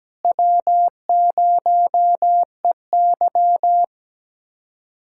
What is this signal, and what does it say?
Morse code, heard as a keyed tone, about 17 wpm, 700 Hz. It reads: W0EY
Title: USA Callsign DE Callsign - 4 Characters - 17wpm